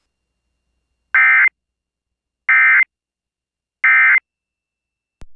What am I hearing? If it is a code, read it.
Emergency Alert System (SAME) tones, heard as harsh buzzing bursts. It is heard from 1.1 to 4.2 s.